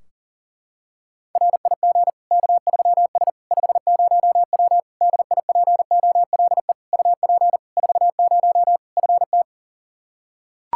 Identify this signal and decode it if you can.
Morse code, heard as a keyed tone, about 40 wpm, 700 Hz. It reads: RIG K3S 50W DIPOLE UP 40 FT
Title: K1USN SST QSO Practice 40wpm